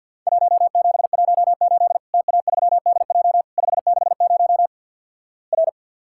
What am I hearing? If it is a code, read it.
Morse code, heard as a keyed tone, about 52 wpm, 700 Hz. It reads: 1718 TA3DJ 560 R
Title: Worked All Europe DX Contest - QTC Exchange - 50wpm